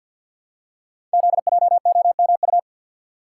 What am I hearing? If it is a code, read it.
Morse code, heard as a keyed tone, about 51 wpm, 700 Hz. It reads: ZJOMU
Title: Code Groups - 5 Letters - 50wpm